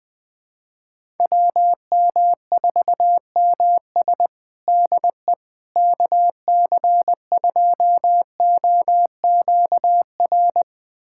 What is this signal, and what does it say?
Morse code, heard as a keyed tone, about 20 wpm, 700 Hz. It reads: WM4MS DE KC2OQR
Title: USA Callsign DE Callsign 20wpm